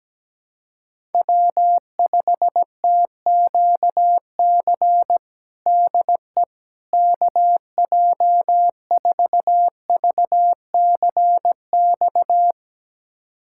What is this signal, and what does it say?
Morse code, heard as a keyed tone, about 17 wpm, 700 Hz. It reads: W5TQC DE KJ4VCX
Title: USA Callsign DE Callsign 17wpm